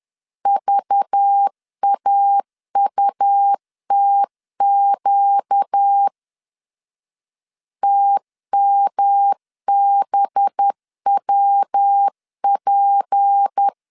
Morse code, heard as a keyed tone, about 10 wpm, 800 Hz. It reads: VAUTQ TMBWP
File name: LEZIONI / 6 PKX0 / 6 P 10wpm 2432